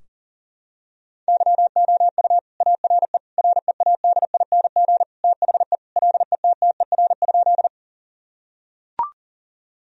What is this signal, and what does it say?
Morse code, heard as a keyed tone, about 40 wpm, 700 Hz. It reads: YOU ARE READING THE LETTER?
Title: Sentences from Top 1000 Words - Review Part 3 - 40wpm